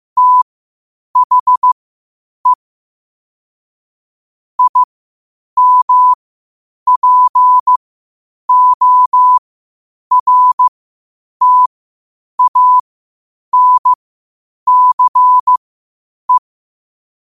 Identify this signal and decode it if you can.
Morse code, heard as a keyed tone, about 15 wpm, spaced as for 8 wpm, 1000 Hz. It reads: THE IMPORTANCE
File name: qotdfiles / 8wpm